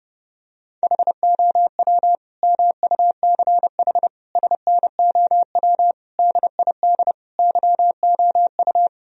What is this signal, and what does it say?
Morse code, heard as a keyed tone, about 32 wpm, 700 Hz. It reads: HOW MUCH SNOW DID YOU
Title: Sentences from Top 1000 Words - Review Part 3 - 30wpm